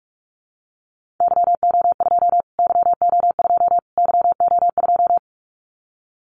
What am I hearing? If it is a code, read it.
Morse code, heard as a keyed tone, about 44 wpm, 700 Hz. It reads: YO2 YO2 YO2
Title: Callsign Prefixes for Top 25 DXCC - Repeated 3x - 45wpm